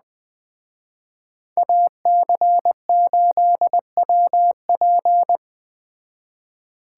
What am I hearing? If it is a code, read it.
Morse code, heard as a keyed tone, about 20 wpm, 700 Hz. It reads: AC8WP
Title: Speed Racing - USA 2x2 Call Signs 17wpm